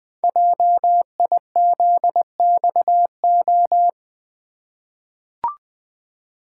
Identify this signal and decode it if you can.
Morse code, heard as a keyed tone, about 20 wpm, 700 Hz. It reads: JIZXO
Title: Code Groups - 5 Letters - 20wpm